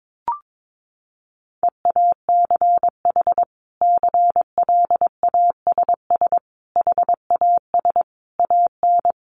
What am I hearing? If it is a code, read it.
Morse code, heard as a keyed tone, about 22 wpm, 700 Hz. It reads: EACH CLASS HAS AN
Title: Sentences from Top 1400 Words 22wpm